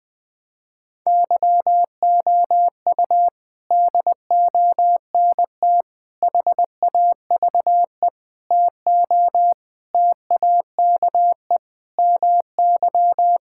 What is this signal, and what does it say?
Morse code, heard as a keyed tone, about 20 wpm, 700 Hz. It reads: YOU DONT HAVE TO TAKE MY
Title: Sets of 8 Words - 20wpm